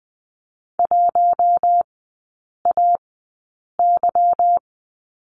Morse code, heard as a keyed tone, about 20 wpm, spaced as for 8 wpm, 700 Hz. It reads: 1AY
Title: 152 Lesson 25e - Letter p - Three Characters 20wpm